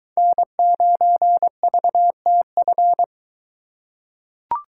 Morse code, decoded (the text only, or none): N9VTF